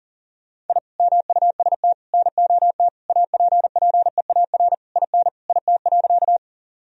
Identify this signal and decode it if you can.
Morse code, heard as a keyed tone, about 37 wpm, 700 Hz. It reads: I MUST NOT APPEAR IN IT.
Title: Sentences from Top 1000 Words - Review Part 2 - 40wpm